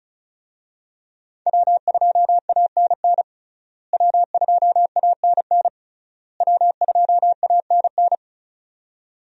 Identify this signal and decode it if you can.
Morse code, heard as a keyed tone, about 34 wpm, 700 Hz. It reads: W2ANN W2ANN W2ANN
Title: USA Call Signs - 5 Characters - Repeated 3x - 35wpm